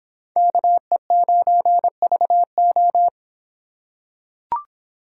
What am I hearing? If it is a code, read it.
Morse code, heard as a keyed tone, about 26 wpm, 700 Hz. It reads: KE9VO